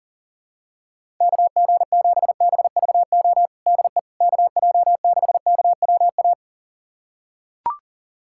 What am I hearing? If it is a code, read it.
Morse code, heard as a keyed tone, about 37 wpm, 700 Hz. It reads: KG7BVO DE KJ6KWA
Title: USA Callsign DE Callsign - 6 Characters - 40wpm